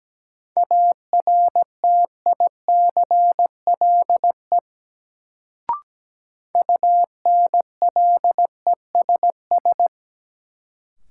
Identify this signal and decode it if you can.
Morse code, heard as a keyed tone, about 17 wpm, 700 Hz. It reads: ARTICLE UNLESS
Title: Sentences from Top 1000 Words - Review Part 5 - New Word Drill - 17wpm